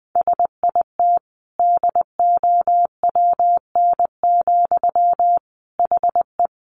Morse code, heard as a keyed tone, about 20 wpm, 700 Hz. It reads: SIT DOWN, HE
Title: Sentences from Top 500 Words 20wpm